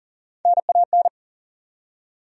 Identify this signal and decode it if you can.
Morse code, heard as a keyed tone, about 37 wpm, 700 Hz. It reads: NAN
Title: USA Common Names - Max 5 Letters - 40wpm